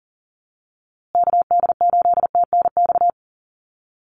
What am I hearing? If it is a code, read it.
Morse code, heard as a keyed tone, about 40 wpm, 700 Hz. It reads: KD8TNX
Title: USA Call Signs Repeated 3x 40wpm